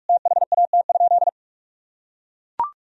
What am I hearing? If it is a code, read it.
Morse code, heard as a keyed tone, about 41 wpm, 700 Hz. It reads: THAT?
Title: Sentences from Top 1000 Words - Review Part 3 - 45wpm